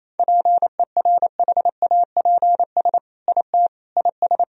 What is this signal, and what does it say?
Morse code, heard as a keyed tone, about 28 wpm, 690 Hz. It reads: PERHAPS IT IS
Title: Sentences from Top 1000 Words - Review Part 2 - 28wpm